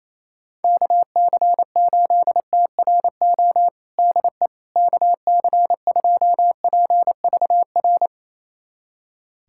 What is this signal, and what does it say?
Morse code, heard as a keyed tone, about 28 wpm, 700 Hz. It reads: KC8TRO DE KC2PVR
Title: USA Callsign DE Callsign - 6 Characters - 28wpm